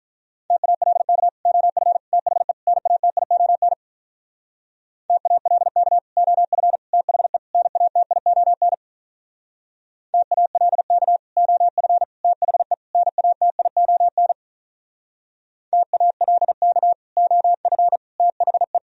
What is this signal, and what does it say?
Morse code, heard as a keyed tone, about 46 wpm, 700 Hz. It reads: TALK OF THE NATION TALK OF THE NATION TALK OF THE NATION TALK OF THE